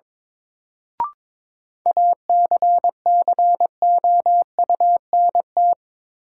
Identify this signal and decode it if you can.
Morse code, heard as a keyed tone, about 22 wpm, 700 Hz. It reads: ACCOUNT
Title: Sentences from Top 900 Words - New Word Drill - 22wpm